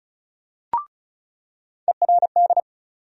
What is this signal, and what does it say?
Morse code, heard as a keyed tone, about 36 wpm, 700 Hz. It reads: ERD
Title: USA Call Sign Postfixes - 3 Letters 35wpm